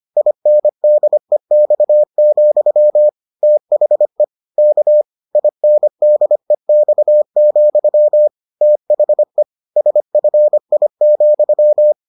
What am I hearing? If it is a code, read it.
Morse code, heard as a keyed tone, about 25 wpm, 600 Hz. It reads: INDEX, THE K INDEX, THE SFI,